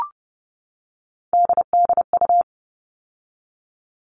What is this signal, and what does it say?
Morse code, heard as a keyed tone, about 30 wpm, 700 Hz. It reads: DDU
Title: All Cut Numbers - 3-Digit 30wpm